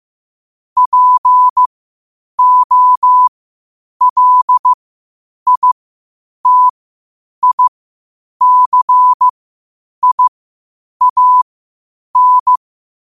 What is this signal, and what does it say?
Morse code, heard as a keyed tone, about 15 wpm, spaced as for 8 wpm, 1000 Hz. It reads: POLITICIAN
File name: qotdfiles / 8wpm